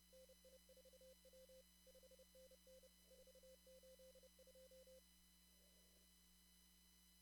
Morse code, heard as a keyed tone, about 30 wpm, 520 Hz, with noise. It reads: NI4W 5NN 482
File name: CW / kt5j wpx cw-op-k5pi